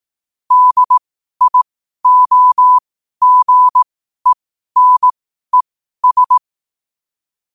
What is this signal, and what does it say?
Morse code, heard as a keyed tone, about 18 wpm, spaced as for 12 wpm, 1000 Hz. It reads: DIOGENES
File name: qotdfiles / 12wpm